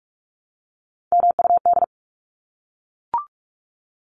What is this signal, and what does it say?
Morse code, heard as a keyed tone, about 44 wpm, 700 Hz. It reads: MUD
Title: Top 1751-2000 Words 45wpm